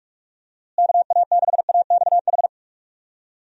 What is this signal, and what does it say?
Morse code, heard as a keyed tone, about 46 wpm, 700 Hz. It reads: KA6AXH
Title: USA Call Signs - 6 Characters - Repeated 3x - 45wpm